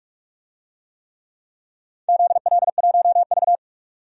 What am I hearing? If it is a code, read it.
Morse code, heard as a keyed tone, about 43 wpm, 700 Hz. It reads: ZL1V
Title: DXCC-100 Call Signs - 4 Characters - 45wpm